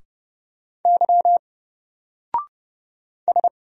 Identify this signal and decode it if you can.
Morse code, heard as a keyed tone, about 30 wpm, 700 Hz. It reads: Y S
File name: Single-Letter-Number-30wpm